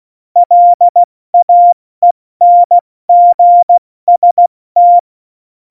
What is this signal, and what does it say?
Morse code, heard as a keyed tone, about 16 wpm, spaced as for 14 wpm, 700 Hz. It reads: LAENGST
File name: Morsekurs / Witze / Witz 29. August 17, Tempo 70 F80